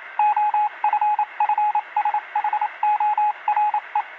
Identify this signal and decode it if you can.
Morse code, heard as a keyed tone, about 27 wpm, 840 Hz, with radio noise. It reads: OFFSHORE